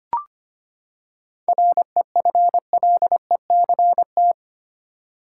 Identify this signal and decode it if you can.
Morse code, heard as a keyed tone, about 25 wpm, 700 Hz. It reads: REFLECT
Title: Seven Letter Words - 25wpm